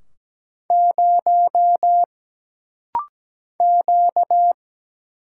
Morse code, heard as a keyed tone, about 17 wpm, 700 Hz. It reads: 0 Q
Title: Single Letter-Number - Rapid-Fire 17wpm